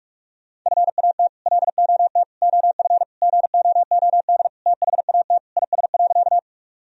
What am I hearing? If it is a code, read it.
Morse code, heard as a keyed tone, about 46 wpm, 700 Hz. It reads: FAT LOT OF GOOD THAT IS.
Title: Sentences from Top 1000 Words - Review Part 4 - 45wpm